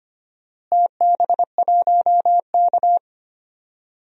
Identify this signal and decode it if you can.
Morse code, heard as a keyed tone, about 25 wpm, 700 Hz. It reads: TB1K
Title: DXCC-100 Call Signs - 4 Characters - Repeated 2x - 25wpm